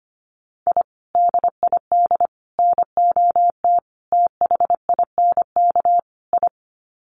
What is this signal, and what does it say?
Morse code, heard as a keyed tone, about 25 wpm, 700 Hz. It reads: I DID NOT THINK I
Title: Sets of 5 Words Encore 25wpm